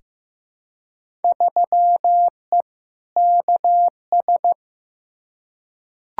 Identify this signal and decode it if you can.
Morse code, heard as a keyed tone, about 15 wpm, 700 Hz. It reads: 3E KS
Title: Field Day Exchanges Encore - 15wpm